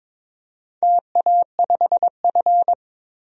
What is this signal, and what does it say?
Morse code, heard as a keyed tone, about 22 wpm, 690 Hz. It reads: TA5F
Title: DXCC-100 Call Signs - 4 Characters - 22wpm